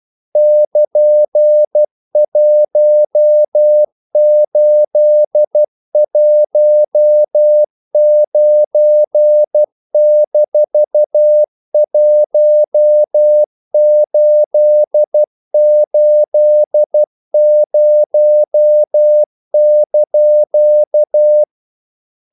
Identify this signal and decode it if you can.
Morse code, heard as a keyed tone, about 12 wpm, 600 Hz. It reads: <KN>1819-1880)
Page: English quotations in CW (telegraphy - morse)